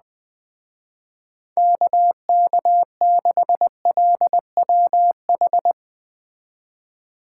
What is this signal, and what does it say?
Morse code, heard as a keyed tone, about 21 wpm, 700 Hz. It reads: KK6LWH
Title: Speed Racing - USA 2x3 Call Signs 17wpm